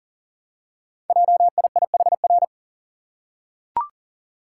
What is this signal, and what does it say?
Morse code, heard as a keyed tone, about 39 wpm, 690 Hz. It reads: JIIHR
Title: Code Groups - 5 Letters - 40wpm